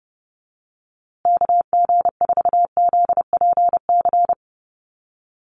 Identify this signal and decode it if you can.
Morse code, heard as a keyed tone, about 30 wpm, 700 Hz. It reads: KG4ZPC
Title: USA Call Signs Repeated 2x 30wpm